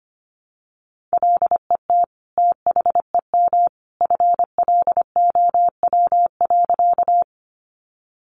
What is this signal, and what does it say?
Morse code, heard as a keyed tone, about 25 wpm, 700 Hz. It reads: LET THEM FLOW.
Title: Sentences from Top 1200 Words 25wpm